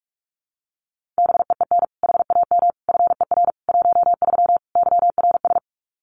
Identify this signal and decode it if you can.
Morse code, heard as a keyed tone, about 44 wpm, 700 Hz. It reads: BEEN HAM FER 13 YRS
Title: Common QSO Phrases 45wpm